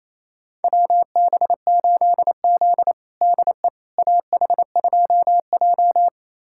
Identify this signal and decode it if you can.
Morse code, heard as a keyed tone, about 28 wpm, 700 Hz. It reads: WB8Z DE AH2J
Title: USA Callsign DE Callsign - 4 Characters - 28wpm